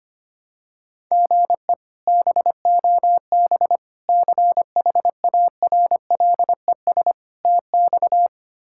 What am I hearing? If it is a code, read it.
Morse code, heard as a keyed tone, about 25 wpm, 700 Hz. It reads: GE BOB CHARLES TX
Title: K1USN SST QSO Practice - 25wpm